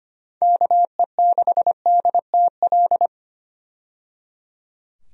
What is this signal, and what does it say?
Morse code, heard as a keyed tone, about 25 wpm, 700 Hz. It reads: KE6DTL
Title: USA Call Signs - 6 Characters - 25wpm